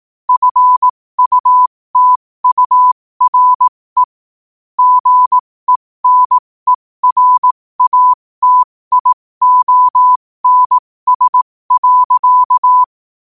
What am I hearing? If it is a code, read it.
Morse code, heard as a keyed tone, about 18 wpm, spaced as for 15 wpm, 1000 Hz. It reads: FUTURE GENERATIONS.
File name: qotdfiles / 15wpm